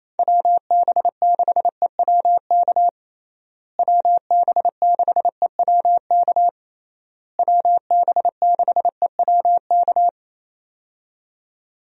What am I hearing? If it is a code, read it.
Morse code, heard as a keyed tone, about 28 wpm, 700 Hz. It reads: WB6EWK WB6EWK WB6EWK
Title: USA Call Signs - 6 Characters - Repeated 3x - 28wpm